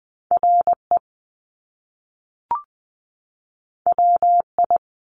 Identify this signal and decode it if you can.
Morse code, heard as a keyed tone, about 20 wpm, 700 Hz. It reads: R E W I
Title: Top Two Letter Combinations in Words 20wpm